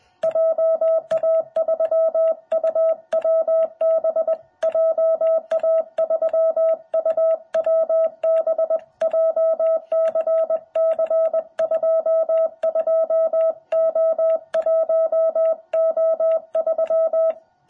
Morse code, heard as a keyed tone, about 21 wpm, 660 Hz. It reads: JA3UWB JA3UWB JCC22O1O3
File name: jh8cbh / 2021cw